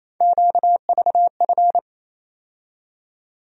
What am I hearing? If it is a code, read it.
Morse code, heard as a keyed tone, about 28 wpm, 700 Hz. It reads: QVF
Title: USA Call Sign Postfixes - 3 Letters 28wpm